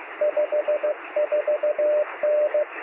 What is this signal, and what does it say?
Morse code, heard as a keyed tone, about 15 wpm, 600 Hz, with radio noise. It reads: 54N
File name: sm7lcb / 2115z g3xdy 23cm